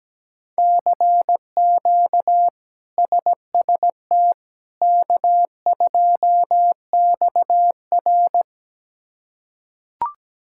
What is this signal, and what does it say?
Morse code, heard as a keyed tone, about 17 wpm, 700 Hz. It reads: CQ SST K2XR
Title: K1USN SST QSO Practice - 17wpm